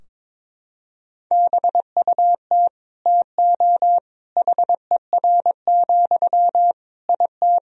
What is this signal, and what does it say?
Morse code, heard as a keyed tone, about 22 wpm, 700 Hz. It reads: BUT TO HER, IT
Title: Sentences from Top 1000 Words - Review Part 2 - 22wpm